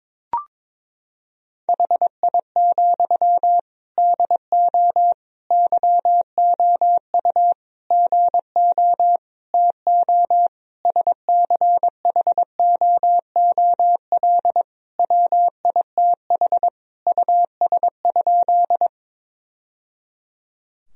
Morse code, heard as a keyed tone, about 22 wpm, 700 Hz. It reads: HI, DO YOU GO TO SCHOOL WITH US?
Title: Sentences from Top 1000 Words - Review Part 3 - 22wpm